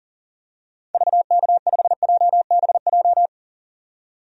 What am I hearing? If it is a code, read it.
Morse code, heard as a keyed tone, about 40 wpm, 700 Hz. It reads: VK5JBJ
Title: DXCC-100 Call Signs - 6 Characters - 40wpm